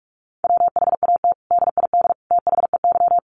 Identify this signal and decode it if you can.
Morse code, heard as a keyed tone, about 44 wpm, 700 Hz. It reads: WHAT DID THEY
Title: Sentences from Top 100 Words 45wpm